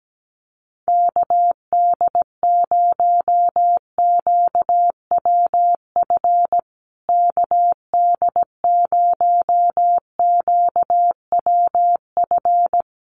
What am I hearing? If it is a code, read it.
Morse code, heard as a keyed tone, about 17 wpm, 700 Hz. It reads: KD0QWF KD0QWF
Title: USA Call Signs Repeated 3x 17wpm